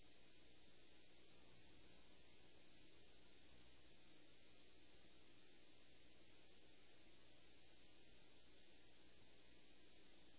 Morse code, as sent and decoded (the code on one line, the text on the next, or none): none